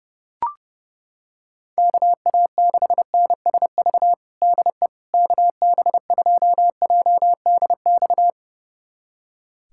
Morse code, heard as a keyed tone, about 31 wpm, 700 Hz. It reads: KA6NSV DE KB2JDX
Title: USA Callsign DE Callsign 30wpm